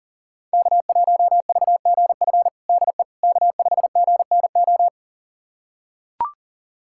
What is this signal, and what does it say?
Morse code, heard as a keyed tone, about 40 wpm, 700 Hz. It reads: K1VGF DE K5GNO
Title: USA Callsign DE Callsign - 5 Characters - 40wpm